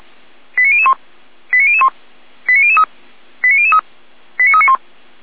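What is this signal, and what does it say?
Five-tone selective call (ZVEI), heard as quick runs of five tones, tones of about 70 ms. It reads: 89001 89001 89003 89003 89391
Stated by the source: ZVEI five-tone selective call 89191, received on an SDR.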